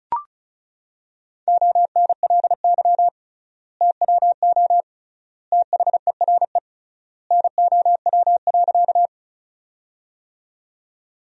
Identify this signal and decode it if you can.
Morse code, heard as a keyed tone, about 36 wpm, 700 Hz. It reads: ONLY TWO THERE NOW.
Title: 3x Word-Spacing - Sentences from Top 200 Words - 35wpm